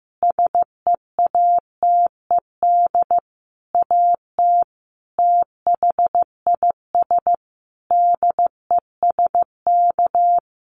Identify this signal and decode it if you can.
Morse code, heard as a keyed tone, about 15 wpm, 700 Hz. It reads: SEATED AT THIS DESK
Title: A Sound Of Thunder 15wpm